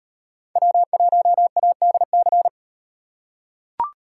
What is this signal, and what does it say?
Morse code, heard as a keyed tone, about 37 wpm, 700 Hz. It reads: W1ADC